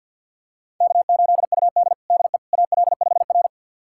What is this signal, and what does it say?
Morse code, heard as a keyed tone, about 52 wpm, 700 Hz. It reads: K8UD DE AL5R
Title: USA Callsign DE Callsign - 4 Characters - 50wpm